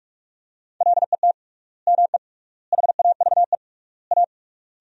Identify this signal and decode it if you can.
Morse code, heard as a keyed tone, about 46 wpm, 700 Hz. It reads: LET ME HAVE A
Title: 3x Word-Spacing - Sentences from Top 300 Words - 45wpm